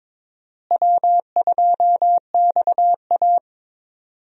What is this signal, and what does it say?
Morse code, heard as a keyed tone, about 22 wpm, 700 Hz. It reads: W2XA
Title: USA Call Signs - 4 Characters - Repeated 2x - 22wpm